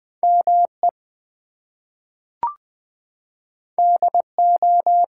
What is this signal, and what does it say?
Morse code, heard as a keyed tone, about 20 wpm, 700 Hz. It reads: ME DO
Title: Two Letter Words - 20wpm